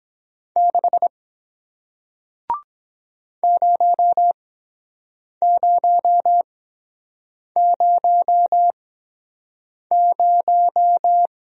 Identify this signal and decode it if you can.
Morse code, heard as a keyed tone, about 22 wpm, spaced as for 7 wpm, 700 Hz. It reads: 6 0000